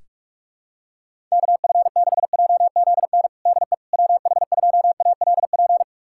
Morse code, heard as a keyed tone, about 41 wpm, 700 Hz. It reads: KF6JZN DE WH2ALP
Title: USA Callsign DE Callsign - 6 Characters - 45wpm